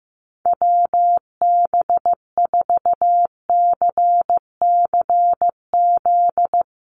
Morse code, heard as a keyed tone, about 15 wpm, 700 Hz. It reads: WB4CCZ
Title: USA Call Signs Repeated 2x 15wpm